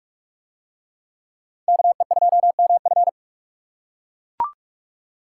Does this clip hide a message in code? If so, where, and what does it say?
Yes: Morse code from 1.7 to 3.1 s, heard as a keyed tone, about 43 wpm, 700 Hz. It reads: KE2MF